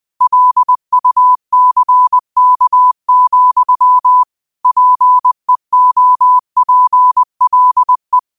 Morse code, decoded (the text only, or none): LUCK, PEOPLE